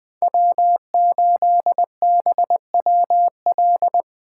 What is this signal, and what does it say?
Morse code, heard as a keyed tone, about 25 wpm, 700 Hz. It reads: W8BWL
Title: Speed Racing - USA 1x3 Call Signs 20wpm